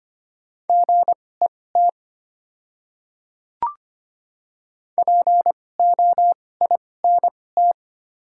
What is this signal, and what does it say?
Morse code, heard as a keyed tone, about 25 wpm, spaced as for 18 wpm, 700 Hz. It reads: GET POINT
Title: 2x Character Spacing - Top 200 Words - 25wpm